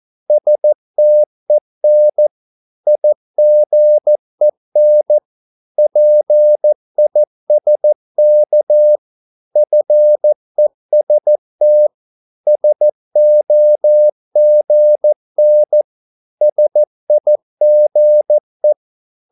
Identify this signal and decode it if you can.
Morse code, heard as a keyed tone, about 14 wpm, 600 Hz. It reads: STEN IGEN PISK FEST SOGN SIGE
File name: Korte ord DK 14wpm